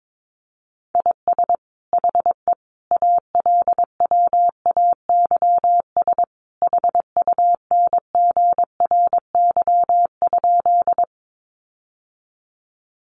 Morse code, heard as a keyed tone, about 22 wpm, 700 Hz. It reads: IS HE ALWAYS HUNGRY?
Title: Sentences from Top 1400 Words 22wpm